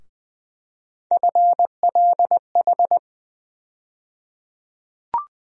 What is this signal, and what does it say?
Morse code, heard as a keyed tone, about 20 wpm, 700 Hz. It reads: FLH
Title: USA Call Sign Postfixes - 3 Letters 20wpm